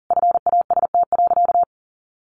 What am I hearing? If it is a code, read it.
Morse code, heard as a keyed tone, about 40 wpm, 700 Hz. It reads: FAST.
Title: Sentences from Top 1300 Words 40wpm